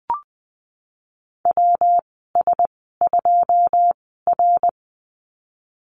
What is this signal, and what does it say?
Morse code, heard as a keyed tone, about 19 wpm, spaced as for 14 wpm, 700 Hz. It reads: WS2R
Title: 193 Lesson 31g - Letter v - Call Signs 20wpm